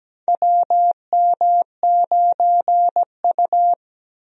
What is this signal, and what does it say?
Morse code, heard as a keyed tone, about 17 wpm, 700 Hz. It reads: WM9U
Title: USA Call Signs - 4 Characters - Repeated 2x - 17wpm